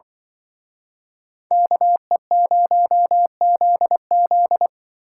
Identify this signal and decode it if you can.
Morse code, heard as a keyed tone, about 25 wpm, 700 Hz. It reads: KE0ZZ
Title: Speed Racing - USA 2x2 Call Signs 20wpm